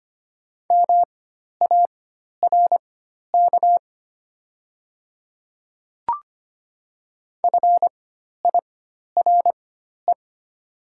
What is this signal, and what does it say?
Morse code, heard as a keyed tone, about 25 wpm, spaced as for 12 wpm, 690 Hz. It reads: MARK FIRE
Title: 4x Character Spacing - Top 500 Words - 25wpm